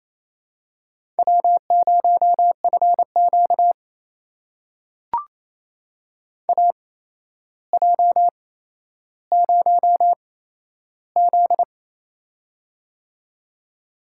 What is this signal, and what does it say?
Morse code, heard as a keyed tone, about 28 wpm, spaced as for 12 wpm, 700 Hz. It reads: W0FQ AJ0Z
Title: USA Call Signs - 4 Characters - 8x Character Spacing - 28wpm